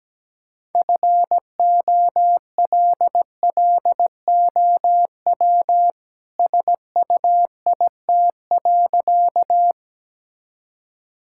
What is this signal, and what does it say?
Morse code, heard as a keyed tone, about 17 wpm, 700 Hz. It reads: FOLLOW SUIT.
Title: Sentences from Top 1600 Words 17wpm